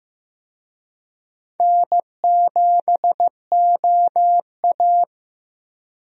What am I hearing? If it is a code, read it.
Morse code, heard as a keyed tone, about 15 wpm, 700 Hz. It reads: N7OA
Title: USA Call Signs - 4 Characters - Repeated 3x - 15wpm